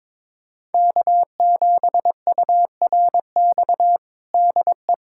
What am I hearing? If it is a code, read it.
Morse code, heard as a keyed tone, about 22 wpm, 700 Hz. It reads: K7URX DE
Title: USA Callsign DE Callsign - 5 Characters - 22wpm